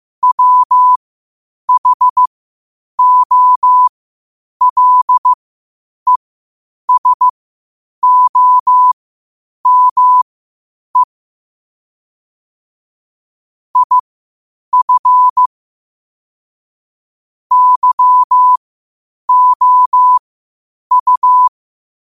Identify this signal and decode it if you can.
Morse code, heard as a keyed tone, about 15 wpm, spaced as for 8 wpm, 1000 Hz. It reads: WHOLESOME IF YOU
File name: qotdfiles / 8wpm